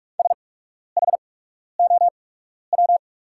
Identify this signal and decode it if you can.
Morse code, heard as a keyed tone, about 44 wpm, spaced as for 12 wpm, 700 Hz. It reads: SHOW
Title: 8x Character Spacing - Top 300 Words - 45wpm